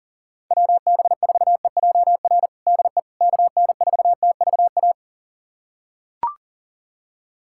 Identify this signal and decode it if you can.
Morse code, heard as a keyed tone, about 40 wpm, 700 Hz. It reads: WB4EJR DE KN4TVA
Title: USA Callsign DE Callsign 40wpm